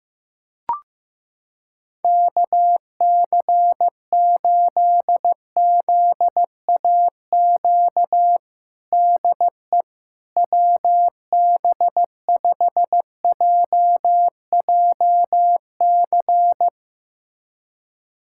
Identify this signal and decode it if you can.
Morse code, heard as a keyed tone, about 15 wpm, 700 Hz. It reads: KC8ZAQ DE WB5JJC
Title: USA Callsign DE Callsign 15wpm